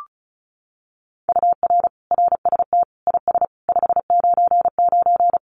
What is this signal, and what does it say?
Morse code, heard as a keyed tone, about 34 wpm, 700 Hz. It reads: UR RST IS 599
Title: Common QSO Phrases Encore 35wpm